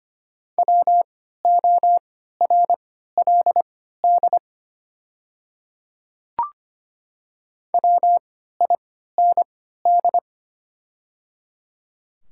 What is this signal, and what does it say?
Morse code, heard as a keyed tone, about 25 wpm, spaced as for 14 wpm, 700 Hz. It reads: WORLD WIND
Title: 3x Character Spacing - Top 500 Words - 25wpm